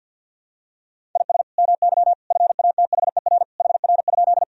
Morse code, heard as a keyed tone, about 52 wpm, 700 Hz. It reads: IS MY FATHER HR?
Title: Sentences from Top 300 Words and Abbreviations 50wpm